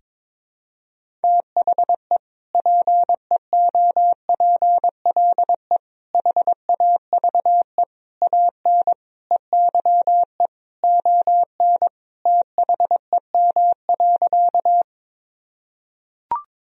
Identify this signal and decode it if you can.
Morse code, heard as a keyed tone, about 22 wpm, 700 Hz. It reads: THE PEOPLE HAVE AN EYE ON THEM.
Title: Sentences from Top 1000 Words - Review Part 1 - 22wpm